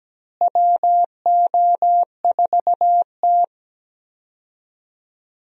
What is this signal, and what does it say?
Morse code, heard as a keyed tone, about 17 wpm, 700 Hz. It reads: WO4T
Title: USA Call Signs - 4 Characters - Repeated 3x - 17wpm